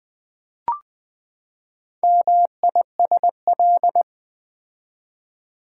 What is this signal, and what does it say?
Morse code, heard as a keyed tone, about 20 wpm, 700 Hz. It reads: MISL
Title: Build A Word - 4 Letters At A Time - 10-Letter Words - 20wpm